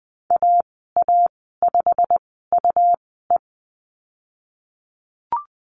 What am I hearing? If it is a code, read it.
Morse code, heard as a keyed tone, about 20 wpm, spaced as for 15 wpm, 700 Hz. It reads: AA5UE
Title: 147 Lesson 24g - Letter y - Call Signs 20wpm